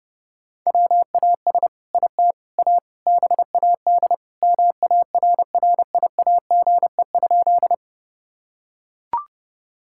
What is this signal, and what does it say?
Morse code, heard as a keyed tone, about 30 wpm, 700 Hz. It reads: WAS IT A BAD MARRIAGE?
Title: Sentences from Top 1000 Words - Review Part 5 - 30wpm